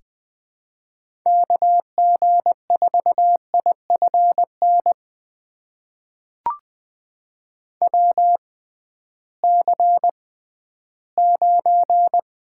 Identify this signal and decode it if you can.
Morse code, heard as a keyed tone, about 20 wpm, spaced as for 13 wpm, 700 Hz. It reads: KG4IFN WC9
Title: USA Call Signs - 6x Character Spacing - 20wpm